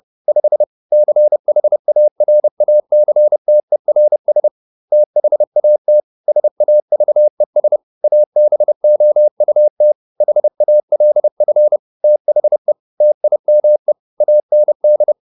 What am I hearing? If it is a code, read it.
Morse code, heard as a keyed tone, about 30 wpm, 600 Hz. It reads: 5 CHARACTERS THAT SAVES ABOUT HALF THE TIME AND